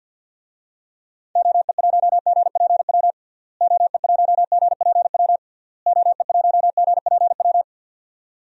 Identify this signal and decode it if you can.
Morse code, heard as a keyed tone, about 49 wpm, 700 Hz. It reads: OE1GPW OE1GPW OE1GPW
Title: DXCC-100 Call Signs - 6 Characters - Repeated 3x - 50wpm